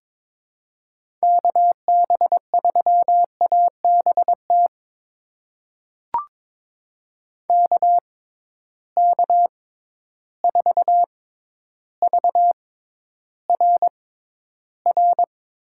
Morse code, heard as a keyed tone, about 22 wpm, spaced as for 11 wpm, 700 Hz. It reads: KB3ABT KK4VRR
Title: USA Call Signs - 6 Characters - 6x Character Spacing - 22wpm